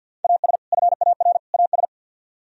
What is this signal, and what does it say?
Morse code, heard as a keyed tone, about 64 wpm, 700 Hz. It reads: AS FAR AS